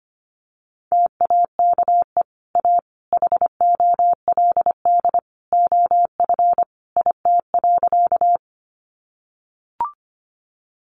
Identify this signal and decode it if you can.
Morse code, heard as a keyed tone, about 25 wpm, 700 Hz. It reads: TAKE A HOLD OF IT.
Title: Sentences from Top 400 Words 25wpm